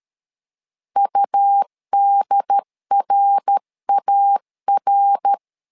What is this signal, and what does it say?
Morse code, heard as a keyed tone, about 13 wpm, 800 Hz. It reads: UDRAR
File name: LEZIONI / 3 MWRO / 3 MWR 12wpm 1711